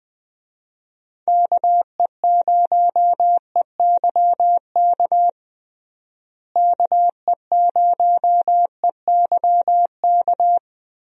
Morse code, heard as a keyed tone, about 20 wpm, 700 Hz. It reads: KE0EYK KE0EYK
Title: USA Call Signs - 6 Characters - Repeated 3x - 20wpm